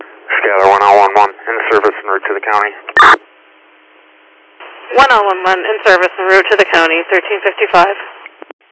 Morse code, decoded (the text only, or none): none